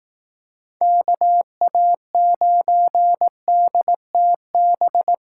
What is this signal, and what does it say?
Morse code, heard as a keyed tone, about 18 wpm, 700 Hz. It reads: KA9DTB
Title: Speed Racing - USA 2x3 Call Signs 15wpm